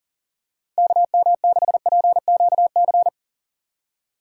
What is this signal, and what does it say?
Morse code, heard as a keyed tone, about 40 wpm, 690 Hz. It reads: KM6PQC
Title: USA Call Signs - 4x Character Spacing - 40wpm